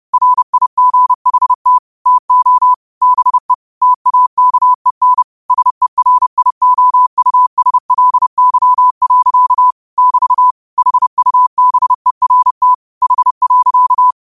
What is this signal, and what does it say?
Morse code, heard as a keyed tone, about 30 wpm, 1000 Hz. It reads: RIGHT TO BE TAKEN SERIOUSLY. <BT> HUBERT H.
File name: qotdfiles / 30wpm